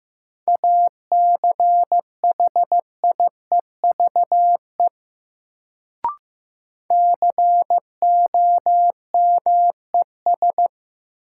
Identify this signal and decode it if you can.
Morse code, heard as a keyed tone, about 15 wpm, 700 Hz. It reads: ACHIEVE COMES
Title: Sentences from Top 1000 Words - Review Part 5 - New Word Drill - 15wpm